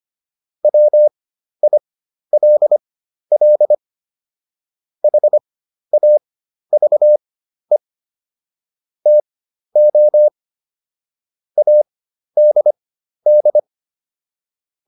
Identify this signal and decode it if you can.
Morse code, heard as a keyed tone, about 25 wpm, spaced as for 12 wpm, 600 Hz. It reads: WILL HAVE TO ADD